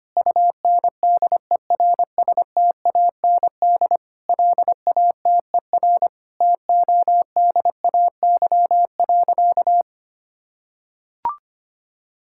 Morse code, heard as a keyed tone, about 25 wpm, 700 Hz. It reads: UNDERSTAND LATER TODAY.
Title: Sentences from Top 1000 Words - Review Part 5 - 25wpm